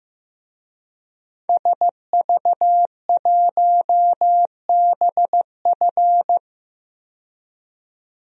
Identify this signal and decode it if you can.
Morse code, heard as a keyed tone, about 15 wpm, 700 Hz. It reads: SV1BF
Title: DXCC-100 Call Signs - 5 Characters - 15wpm